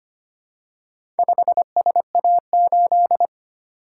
Morse code, heard as a keyed tone, about 25 wpm, 700 Hz. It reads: 5SA8
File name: practice / 25wpm / Four Letter-Number Combinations - 25wpm